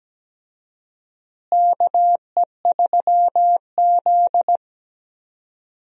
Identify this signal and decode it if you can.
Morse code, heard as a keyed tone, about 17 wpm, 700 Hz. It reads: KE3Z
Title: USA Call Signs - 4 Characters - Repeated 3x - 17wpm